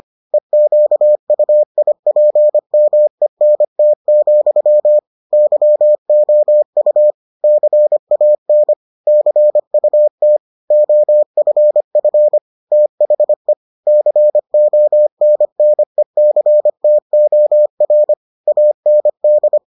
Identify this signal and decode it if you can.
Morse code, heard as a keyed tone, about 25 wpm, 600 Hz. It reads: EQUIPMENT, YOU CAN CUT OFF THE CONNECTOR AND